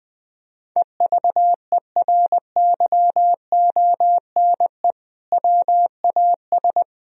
Morse code, heard as a keyed tone, about 21 wpm, 700 Hz. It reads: EVERYONE WAS
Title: Sentences from Top 1000 Words - Review Part 1 - 20wpm